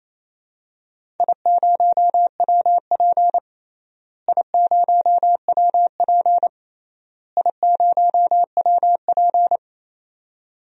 Morse code, heard as a keyed tone, about 28 wpm, 700 Hz. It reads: I0WP I0WP I0WP
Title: DXCC-100 Call Signs - 4 Characters - Repeated 3x - 28wpm